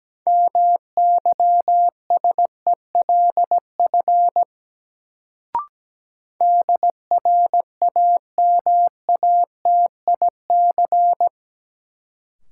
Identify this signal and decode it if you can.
Morse code, heard as a keyed tone, about 17 wpm, 700 Hz. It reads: MYSELF DRAMATIC
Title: Sentences from Top 1300 Words - New Word Drill - 17wpm